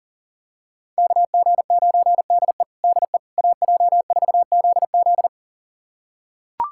Morse code, heard as a keyed tone, about 40 wpm, 700 Hz. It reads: KG9DE DE AJ4ZZ
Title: USA Callsign DE Callsign - 5 Characters - 40wpm